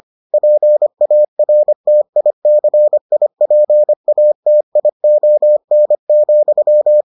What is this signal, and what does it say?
Morse code, heard as a keyed tone, about 25 wpm, 600 Hz. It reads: PARTICIPATION,